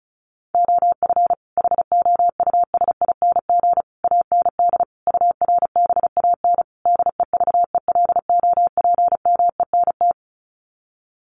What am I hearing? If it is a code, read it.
Morse code, heard as a keyed tone, about 34 wpm, 700 Hz. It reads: OF HOUSING AND URBAN DEVELOPMENT
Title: Sets of 5 Words Encore2 35wpm